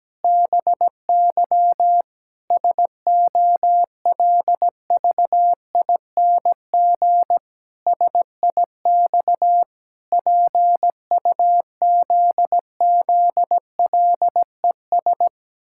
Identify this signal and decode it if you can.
Morse code, heard as a keyed tone, about 17 wpm, 700 Hz. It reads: BY SOLVING SIX PUZZLES
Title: Pangram Challenge v2 - 17wpm